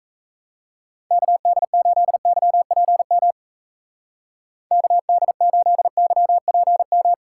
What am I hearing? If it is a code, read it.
Morse code, heard as a keyed tone, about 35 wpm, 700 Hz. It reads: KD8YPM KD8YPM